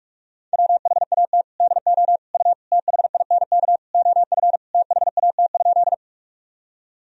Morse code, heard as a keyed tone, about 46 wpm, 700 Hz. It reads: WHAT DO U THINK OF THAT?
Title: Sentences from Top 200 Words and Abbreviations 45wpm